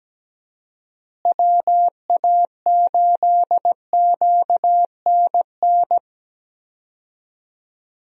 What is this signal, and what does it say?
Morse code, heard as a keyed tone, about 17 wpm, 700 Hz. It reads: WA8QNN
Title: USA Call Signs - 6 Characters - Repeated 3x - 17wpm